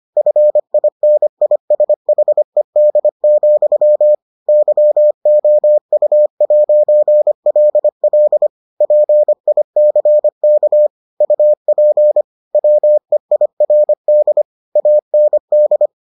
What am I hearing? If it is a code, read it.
Morse code, heard as a keyed tone, about 25 wpm, 600 Hz. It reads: FINISHED, YOU'LL PICK UP WEIRD AND